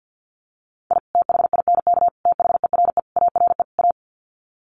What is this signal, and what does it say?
Morse code, heard as a keyed tone, about 48 wpm, 700 Hz. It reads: I THINK THERE ARE A
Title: Sets of 5 Words Encore2 50wpm